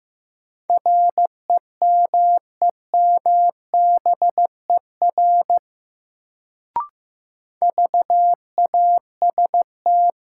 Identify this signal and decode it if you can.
Morse code, heard as a keyed tone, about 15 wpm, 700 Hz. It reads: REMEMBER VAST
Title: Sets of 4 Words - New Word Drill - 15wpm